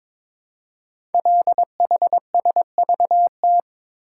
Morse code, heard as a keyed tone, about 22 wpm, 700 Hz. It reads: LHSVT
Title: Code Groups - 5 Letters - 22wpm